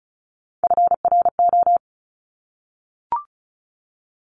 Morse code, heard as a keyed tone, about 36 wpm, 700 Hz. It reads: FRO
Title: Top Three Letter Combinations in Words 35wpm